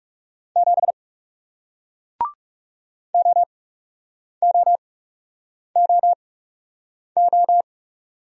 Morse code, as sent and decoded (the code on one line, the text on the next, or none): --... / --- --- --- ---
7 OOOO